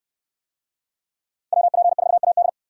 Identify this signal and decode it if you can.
Morse code, heard as a keyed tone, about 68 wpm, 700 Hz. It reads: WP4AD